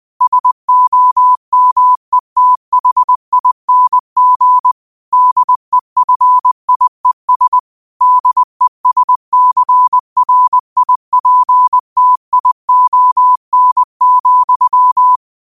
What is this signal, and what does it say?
Morse code, heard as a keyed tone, about 20 wpm, 1000 Hz. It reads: SOMETHING DEFIES DESCRIPTION,